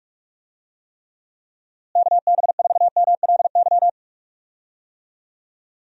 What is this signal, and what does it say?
Morse code, heard as a keyed tone, about 46 wpm, 700 Hz. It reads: KB4MLY
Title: USA Call Signs - 6 Characters - Repeated 2x - 45wpm